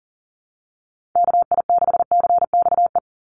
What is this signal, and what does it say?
Morse code, heard as a keyed tone, about 37 wpm, 700 Hz. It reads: KI6CXE
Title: USA Call Signs 40wpm